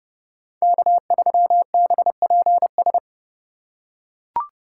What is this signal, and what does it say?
Morse code, heard as a keyed tone, about 28 wpm, 700 Hz. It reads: K3BPS